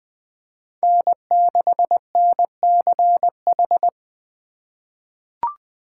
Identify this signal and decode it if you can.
Morse code, heard as a keyed tone, about 20 wpm, 700 Hz. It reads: N6NCH
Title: USA Call Signs - 5 Characters - 2x Character Spacing - 20wpm